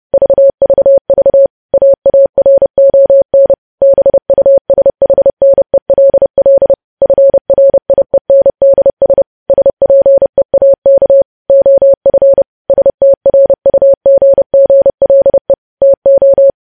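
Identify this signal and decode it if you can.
Morse code, heard as a keyed tone, about 30 wpm, 550 Hz. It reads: VVV AARON BUSHNELL FRIENDS SPEAK OF STRUGGLE TO
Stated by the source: Morse code practice of news headlines at 30.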